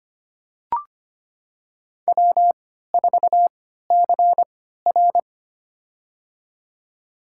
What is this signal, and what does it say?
Morse code, heard as a keyed tone, about 25 wpm, spaced as for 14 wpm, 700 Hz. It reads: W4CR
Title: USA Call Signs - 4 Characters - 3x Character Spacing - 25wpm